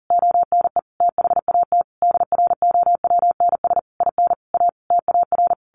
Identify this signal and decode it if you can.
Morse code, heard as a keyed tone, about 40 wpm, 700 Hz. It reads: ONE THAT DROWNS IN A TAR
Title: A Sound Of Thunder 40wpm